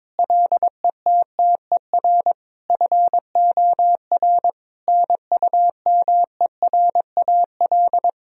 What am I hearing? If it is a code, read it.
Morse code, heard as a keyed tone, about 22 wpm, 700 Hz. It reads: LETTER FOR NUMERAL
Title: Sentences from Top 1000 Words - Review Part 2 - 22wpm